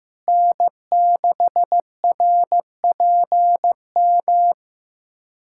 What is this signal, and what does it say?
Morse code, heard as a keyed tone, about 15 wpm, 700 Hz. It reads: N6RPM
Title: USA Callsign DE Callsign - 5 Characters - 15wpm